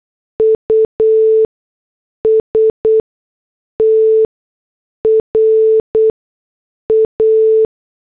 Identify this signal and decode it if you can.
Morse code, heard as a keyed tone, about 8 wpm, spaced as for 6 wpm, 440 Hz. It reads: USTRA